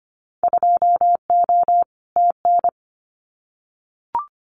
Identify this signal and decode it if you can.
Morse code, heard as a keyed tone, about 25 wpm, 700 Hz. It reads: 2O TN
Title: Winter Field Day Exchanges 25wpm